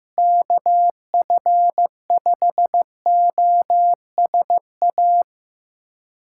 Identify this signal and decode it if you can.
Morse code, heard as a keyed tone, about 15 wpm, 700 Hz. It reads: KF5OSA
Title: USA Callsign DE Callsign - 6 Characters - 15wpm